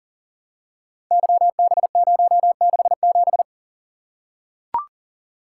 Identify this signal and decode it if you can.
Morse code, heard as a keyed tone, about 39 wpm, 700 Hz. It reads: YB067
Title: Code Groups - 5 Letter:Numbers - 40wpm